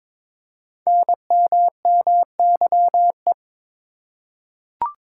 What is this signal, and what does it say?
Morse code, heard as a keyed tone, about 22 wpm, 700 Hz. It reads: NMMYE
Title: Code Groups - 5 Letters - 22wpm